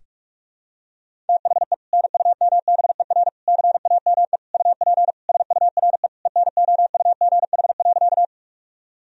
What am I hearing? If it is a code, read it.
Morse code, heard as a keyed tone, about 46 wpm, 700 Hz. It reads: THE NUMBER CAME UP SURE ENOUGH.
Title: Sentences from Top 1000 Words - Review Part 1 - 45wpm